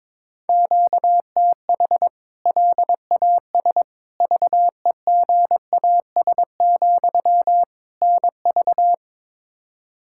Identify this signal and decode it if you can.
Morse code, heard as a keyed tone, about 22 wpm, 700 Hz. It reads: QTH LAS VEGAS, NV
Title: K1USN SST QSO Practice 22wpm